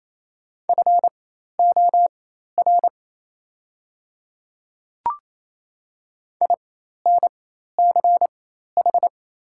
Morse code, heard as a keyed tone, about 27 wpm, spaced as for 13 wpm, 700 Hz. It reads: FOR INCH
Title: 4x Character Spacing - Top 500 Words - 28wpm